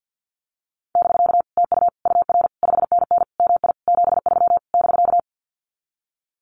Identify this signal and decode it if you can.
Morse code, heard as a keyed tone, about 50 wpm, 700 Hz. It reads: <BK> TU UR 5NN MI 73 <BK>
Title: Common QSO Phrases Encore 50wpm